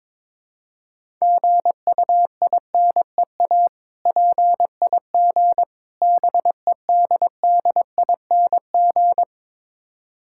Morse code, heard as a keyed tone, about 22 wpm, 690 Hz. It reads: GUINEA PIG BEDDING
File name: Difficult Words with Hints - 22wpm